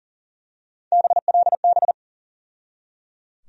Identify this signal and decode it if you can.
Morse code, heard as a keyed tone, about 38 wpm, 700 Hz. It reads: BLB
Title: USA Call Sign Postfixes - 3 Letters 40wpm